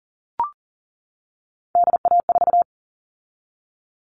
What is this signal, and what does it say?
Morse code, heard as a keyed tone, about 40 wpm, 700 Hz. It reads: DA4
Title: All Cut Numbers - 3-Digit 40wpm